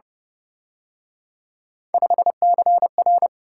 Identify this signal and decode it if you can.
Morse code, heard as a keyed tone, about 31 wpm, 700 Hz. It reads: 5CR
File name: Speed-Racing - Three Letter-Number Combinations - 25wpm